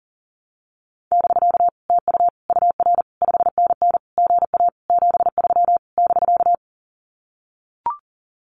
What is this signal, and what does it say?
Morse code, heard as a keyed tone, about 40 wpm, 700 Hz. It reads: <BK> TU UR 5NN GA 73 <BK>
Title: Common QSO Phrases Encore 40wpm